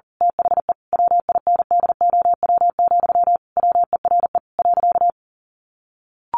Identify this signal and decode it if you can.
Morse code, heard as a keyed tone, about 40 wpm, 700 Hz. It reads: THE WINDOW, WERE.
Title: A Sound Of Thunder 40wpm